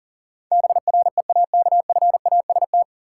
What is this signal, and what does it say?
Morse code, heard as a keyed tone, about 40 wpm, 700 Hz. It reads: BREAKFAST
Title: Sentences from Top 1500 Words - New Word Drill Alt - 40wpm